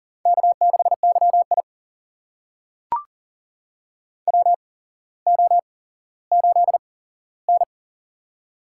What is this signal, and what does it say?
Morse code, heard as a keyed tone, about 40 wpm, spaced as for 17 wpm, 700 Hz. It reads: K6YI WO8N
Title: USA Call Signs - 4 Characters - 8x Character Spacing - 40wpm